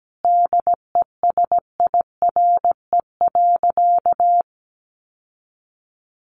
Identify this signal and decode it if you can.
Morse code, heard as a keyed tone, about 17 wpm, 700 Hz. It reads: DESIRE.
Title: Sentences from Top 1100 Words 17wpm